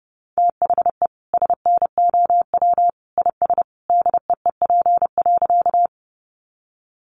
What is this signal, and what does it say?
Morse code, heard as a keyed tone, about 30 wpm, 700 Hz. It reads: THE SNOW IS DEEP.
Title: Sentences from Top 500 Words 30wpm